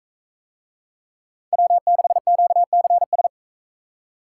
Morse code, heard as a keyed tone, about 41 wpm, 700 Hz. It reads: W6QCS